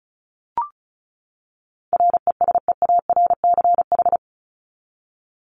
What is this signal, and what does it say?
Morse code, heard as a keyed tone, about 34 wpm, 700 Hz. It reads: RESEARCH
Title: Top 501-750 Words 35wpm